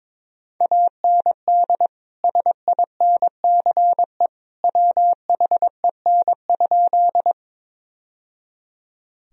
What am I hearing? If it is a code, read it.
Morse code, heard as a keyed tone, about 22 wpm, 700 Hz. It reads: AND SINCE WHEN?
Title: Sentences from Top 1000 Words - Review Part 1 - 22wpm